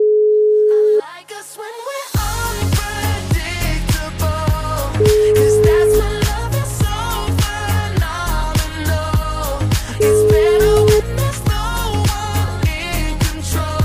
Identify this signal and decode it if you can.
Ringback tone (Europe, 425 Hz), with noise.